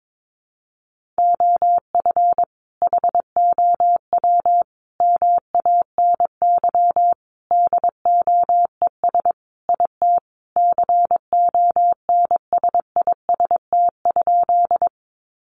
Morse code, heard as a keyed tone, about 22 wpm, 700 Hz. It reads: OF HOW MANY DOES IT CONSIST?
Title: Sentences from Top 1400 Words 22wpm